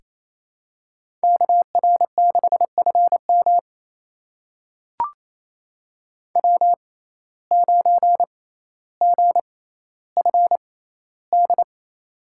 Morse code, heard as a keyed tone, about 28 wpm, spaced as for 14 wpm, 700 Hz. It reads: KR6FM W9GFD
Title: USA Call Signs - 5 Characters - 6x Character Spacing - 28wpm